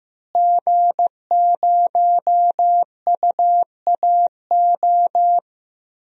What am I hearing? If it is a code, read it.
Morse code, heard as a keyed tone, about 16 wpm, 700 Hz. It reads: G0UAO
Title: DXCC-100 Call Signs - 5 Characters - Repeated 2x - 15wpm